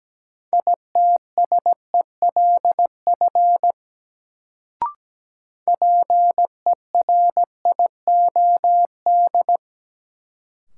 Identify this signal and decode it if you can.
Morse code, heard as a keyed tone, about 17 wpm, 700 Hz. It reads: ITSELF PERIOD
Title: Sentences from Top 800 Words - New Word Drill - 17wpm